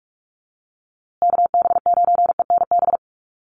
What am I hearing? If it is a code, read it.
Morse code, heard as a keyed tone, about 44 wpm, 700 Hz. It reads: KB9ENB
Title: USA Call Signs Repeated 3x 45wpm